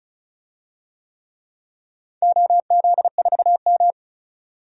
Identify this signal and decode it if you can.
Morse code, heard as a keyed tone, about 34 wpm, 700 Hz. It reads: OZ4M
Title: DXCC-100 Call Signs - 4 Characters - 35wpm